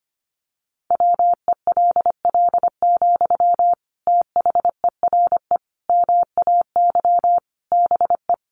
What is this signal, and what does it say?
Morse code, heard as a keyed tone, about 25 wpm, 700 Hz. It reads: WELL, THERE MAY BE
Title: Sentences from Top 1300 Words 25wpm